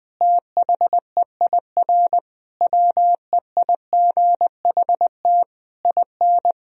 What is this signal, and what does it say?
Morse code, heard as a keyed tone, about 20 wpm, 700 Hz. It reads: THEIR WEIGHT IN
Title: Sentences from Top 1000 Words - Review Part 3 - 20wpm